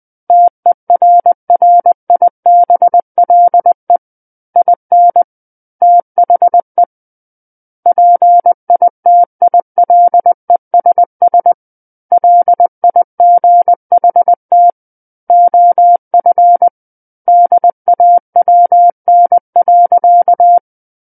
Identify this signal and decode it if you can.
Morse code, heard as a keyed tone, about 20 wpm, 700 Hz. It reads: TERRIBLE IN THE PITILESS LIGHT OF DAWN.
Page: SKCC Learning Center - Morse Code Practice Files